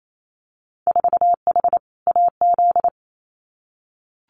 Morse code, heard as a keyed tone, about 28 wpm, 700 Hz. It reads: 4H AZ
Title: Winter Field Day Exchanges Encore 28wpm